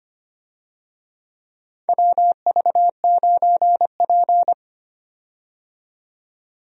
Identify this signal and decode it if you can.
Morse code, heard as a keyed tone, about 25 wpm, 700 Hz. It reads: WV9P
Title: USA Call Signs - 4 Characters - 25wpm